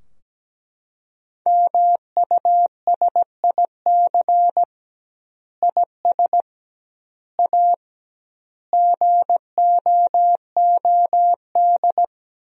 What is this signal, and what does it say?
Morse code, heard as a keyed tone, about 17 wpm, 700 Hz. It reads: MUSIC IS A GOOD
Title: 2x Word-Spacing - Sentences from Top 300 Words - 17wpm